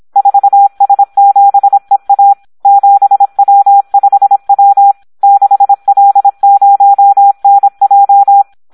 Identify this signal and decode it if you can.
Morse code, heard as a keyed tone, about 26 wpm, 800 Hz. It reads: 4S7EA 7W5W 6L0NJ
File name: Audio practica CW / SD 1 26